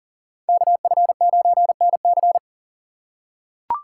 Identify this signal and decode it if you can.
Morse code, heard as a keyed tone, about 40 wpm, 700 Hz. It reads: KF9NC
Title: USA Call Signs - 5 Characters - 4x Character Spacing - 40wpm